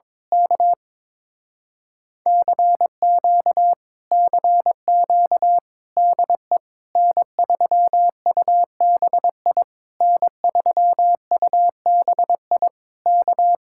Morse code, heard as a keyed tone, about 22 wpm, 700 Hz. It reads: K CQ CQ DE N3UBI N3UBI K
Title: Speed-Racing Common QSO Phrases Encore - 22wpm